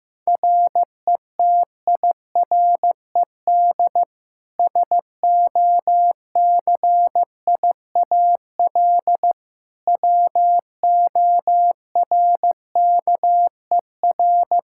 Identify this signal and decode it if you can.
Morse code, heard as a keyed tone, about 15 wpm, 700 Hz. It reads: RETIRED SOCIAL WORKER
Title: K1USN SST QSO Practice 15wpm